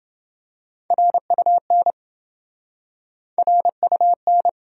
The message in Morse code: RUN RUN

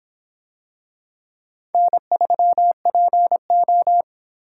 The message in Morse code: N3PO

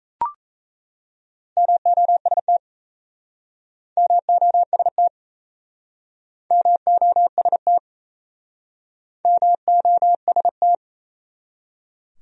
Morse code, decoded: MOST MOST MOST MOST

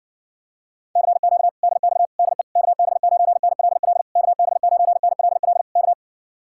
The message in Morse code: CQ DX DE KB9NLL KB9NLL K

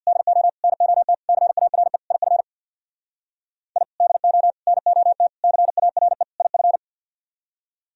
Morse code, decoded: DO NOT CARE IF I DO NOT CARE IF

